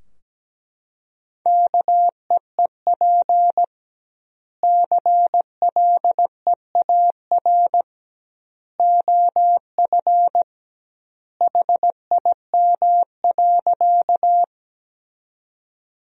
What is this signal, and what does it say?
Morse code, heard as a keyed tone, about 17 wpm, 700 Hz. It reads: KEEP CLEAR OF HIM.
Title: 2x Word-Spacing - Sentences from Top 500 Words - 17wpm